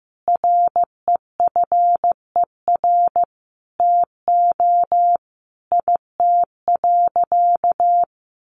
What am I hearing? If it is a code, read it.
Morse code, heard as a keyed tone, about 15 wpm, 700 Hz. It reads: REFER TO IT.
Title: Sentences from Top 1000 Words 15wpm